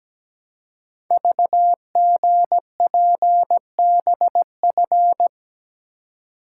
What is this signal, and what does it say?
Morse code, heard as a keyed tone, about 17 wpm, 700 Hz. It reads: VGPBF
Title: Code Groups - 5 Letters - 17wpm